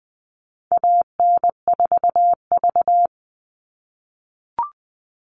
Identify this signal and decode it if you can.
Morse code, heard as a keyed tone, about 20 wpm, 700 Hz. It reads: AN4V
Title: All Cut Numbers - 4-Digit 20wpm